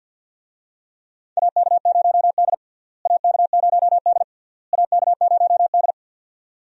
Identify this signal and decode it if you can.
Morse code, heard as a keyed tone, about 53 wpm, 700 Hz. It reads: AK0D AK0D AK0D
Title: USA Call Signs - 4 Characters - Repeated 3x - 50wpm